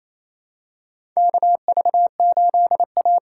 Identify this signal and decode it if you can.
Morse code, heard as a keyed tone, about 28 wpm, 700 Hz. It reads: KV8A
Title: USA Call Signs - 4 Characters - 8x Character Spacing - 28wpm